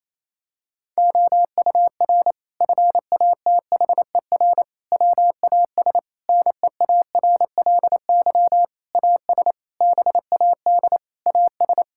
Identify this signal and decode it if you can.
Morse code, heard as a keyed tone, about 28 wpm, 700 Hz. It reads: OUR FATHER WAS NEARLY AS BAD AS
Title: Sentences from Top 1600 Words 28wpm